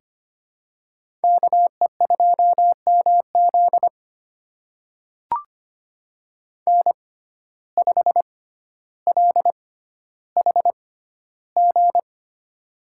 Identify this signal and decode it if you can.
Morse code, heard as a keyed tone, about 25 wpm, spaced as for 13 wpm, 700 Hz. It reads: KE2MZ N5LHG
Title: USA Call Signs - 5 Characters - 6x Character Spacing - 25wpm